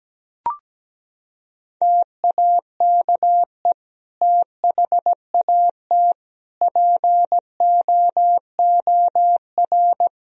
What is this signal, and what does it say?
Morse code, heard as a keyed tone, about 17 wpm, 700 Hz. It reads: TAKE THAT POOR
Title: Sentences from Top 1000 Words - Review Part 2 - 17wpm